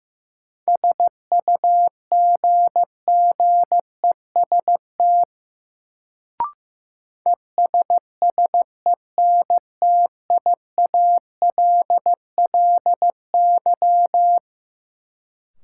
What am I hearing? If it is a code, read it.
Morse code, heard as a keyed tone, about 15 wpm, 700 Hz. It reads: SUGGEST ESSENTIALLY
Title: Sentences from Top 1300 Words - New Word Drill Alt - 15wpm